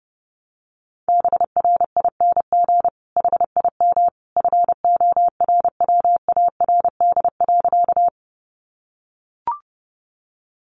Morse code, heard as a keyed tone, about 30 wpm, 700 Hz. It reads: BRING HIM FORWARD.
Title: Sentences from Top 1300 Words 30wpm